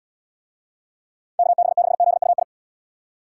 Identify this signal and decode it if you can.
Morse code, heard as a keyed tone, about 75 wpm, 700 Hz. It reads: KB6ZUI